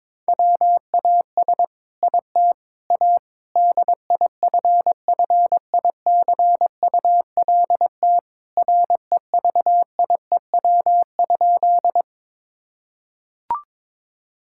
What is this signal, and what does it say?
Morse code, heard as a keyed tone, about 21 wpm, 700 Hz. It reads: WAS IT A DIFFICULT REVIEW?
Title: Sentences from Top 1000 Words - Review Part 6 - 22wpm